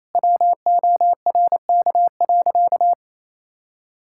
Morse code, heard as a keyed tone, about 28 wpm, 700 Hz. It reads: WORK.
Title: Sentences from Top 1600 Words 28wpm